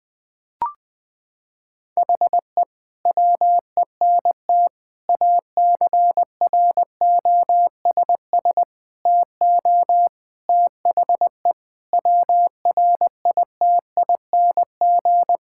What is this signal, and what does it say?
Morse code, heard as a keyed tone, about 20 wpm, 700 Hz. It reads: HE WENT ACROSS TO THE WRITING
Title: Sentences from Top 1000 Words - Review Part 5 - 20wpm